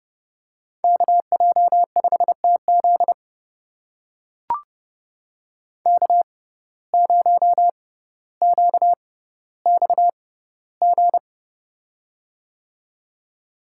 Morse code, heard as a keyed tone, about 32 wpm, spaced as for 16 wpm, 700 Hz. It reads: KJ5TZ K0QXG
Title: USA Call Signs - 5 Characters - 6x Character Spacing - 30wpm